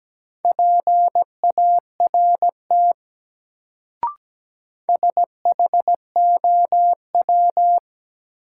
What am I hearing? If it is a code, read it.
Morse code, heard as a keyed tone, about 17 wpm, 700 Hz. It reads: PART SHOW
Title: Sentences from Top 200 Words - New Word Drill - 17wpm